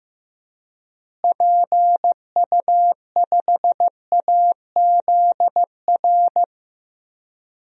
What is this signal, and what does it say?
Morse code, heard as a keyed tone, about 15 wpm, 700 Hz. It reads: PU5AZR
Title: DXCC-100 Call Signs - 6 Characters - Repeated 3x - 15wpm